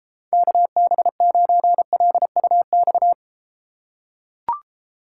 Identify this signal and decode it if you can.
Morse code, heard as a keyed tone, about 33 wpm, 700 Hz. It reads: KB9LUX